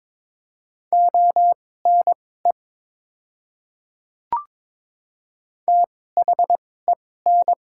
Morse code, heard as a keyed tone, about 22 wpm, spaced as for 16 wpm, 700 Hz. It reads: ONE THEN
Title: 2x Character Spacing - Top 100 Words - 22wpm